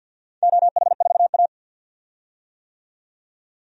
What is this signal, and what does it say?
Morse code, heard as a keyed tone, about 46 wpm, 700 Hz. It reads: OH4A
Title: DXCC-100 Call Signs - 4 Characters - 50wpm